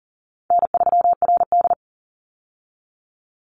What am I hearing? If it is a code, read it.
Morse code, heard as a keyed tone, about 40 wpm, 700 Hz. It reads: N3RD
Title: Call Signs - 40wpm